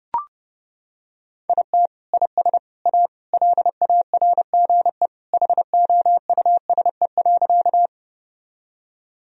Morse code, heard as a keyed tone, about 30 wpm, 700 Hz. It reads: IT IS A LARGE HOUSE.
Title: Sentences from Top 1000 Words - Review Part 1 - 30wpm